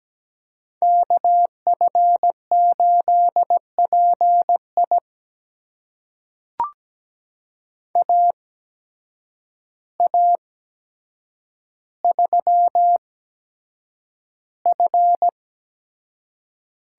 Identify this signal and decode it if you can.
Morse code, heard as a keyed tone, about 17 wpm, spaced as for 8 wpm, 700 Hz. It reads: KF8PI AA3F
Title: USA Call Signs - 5 Characters - 8x Character Spacing - 17wpm